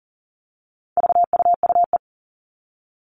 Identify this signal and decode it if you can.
Morse code, heard as a keyed tone, about 40 wpm, 700 Hz. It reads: VUUE
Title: All Cut Numbers - 4-Digit 40wpm